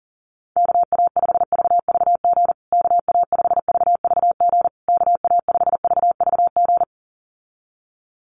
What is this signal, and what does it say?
Morse code, heard as a keyed tone, about 40 wpm, 700 Hz. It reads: KA5VVG KA5VVG KA5VVG
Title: USA Call Signs Repeated 3x 40wpm